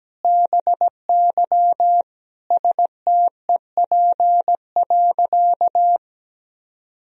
Morse code, heard as a keyed tone, about 17 wpm, 700 Hz. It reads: BY STEP.
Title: Sentences from Top 1000 Words - Review Part 2 - 17wpm